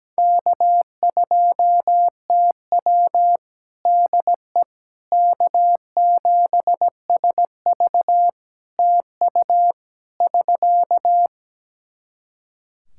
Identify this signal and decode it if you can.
Morse code, heard as a keyed tone, about 17 wpm, 700 Hz. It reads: K2TW DE K7SV TU <SK>
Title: K1USN SST QSO Practice 17wpm